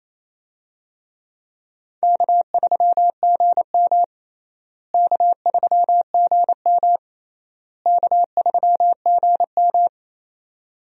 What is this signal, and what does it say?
Morse code, heard as a keyed tone, about 28 wpm, 700 Hz. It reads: K3GM K3GM K3GM
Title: USA Call Signs - 4 Characters - Repeated 3x - 28wpm